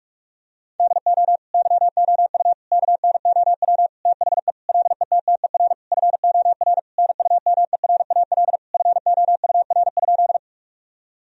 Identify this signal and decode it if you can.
Morse code, heard as a keyed tone, about 46 wpm, 700 Hz. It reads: DO YOU KNOW THE LETTER FOR NUMERAL FOUR?
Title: Sentences from Top 1000 Words - Review Part 2 - 45wpm